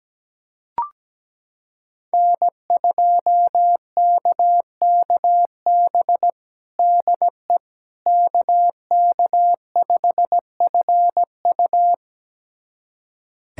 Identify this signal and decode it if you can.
Morse code, heard as a keyed tone, about 17 wpm, 700 Hz. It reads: N2KKB DE KK5FU
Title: USA Callsign DE Callsign 17wpm